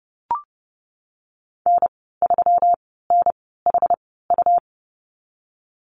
Morse code, heard as a keyed tone, about 30 wpm, spaced as for 17 wpm, 700 Hz. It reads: N3NHU
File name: Lesson 37g - Letter x - Call Signs 30wpm